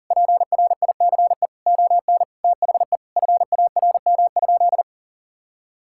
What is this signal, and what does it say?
Morse code, heard as a keyed tone, about 40 wpm, 700 Hz. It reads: PRICE ON THE FARM?
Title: Sentences from Top 1000 Words - Review Part 6 - 40wpm